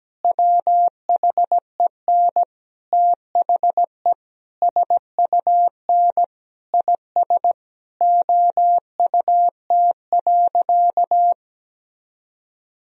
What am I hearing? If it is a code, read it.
Morse code, heard as a keyed tone, about 17 wpm, 700 Hz. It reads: WHEN THE SUN IS OUT.
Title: Sentences from Top 1000 Words - Review Part 1 - 17wpm